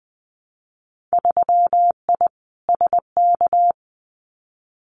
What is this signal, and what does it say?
Morse code, heard as a keyed tone, about 20 wpm, 700 Hz. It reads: 3I SK
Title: Winter Field Day Exchanges Encore 20wpm